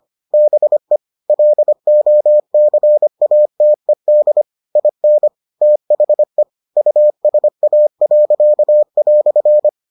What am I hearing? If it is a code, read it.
Morse code, heard as a keyed tone, about 25 wpm, 600 Hz. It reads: BE LOCATED IN THE USA."